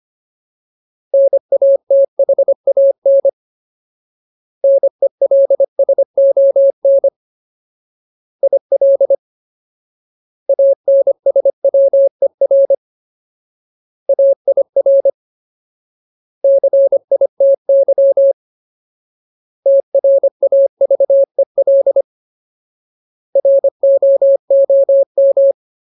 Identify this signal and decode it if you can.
Morse code, heard as a keyed tone, about 25 wpm, 550 Hz. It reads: NATHAN NELSON IL ANSWER AIR CITY TRAVEL ROOM
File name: Ultimate Word List / 25WPM Ultimate